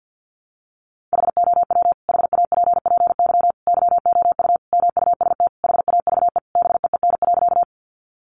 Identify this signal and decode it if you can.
Morse code, heard as a keyed tone, about 50 wpm, 700 Hz. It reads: HOW HAPPY YOU MUST HAVE BEEN.
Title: Sentences from Top 1000 Words 50wpm